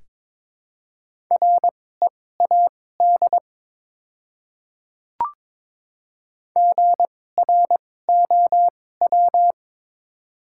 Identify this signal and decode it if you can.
Morse code, heard as a keyed tone, about 22 wpm, spaced as for 16 wpm, 700 Hz. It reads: READ GROW
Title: 2x Character Spacing - Top 300 Words - 22wpm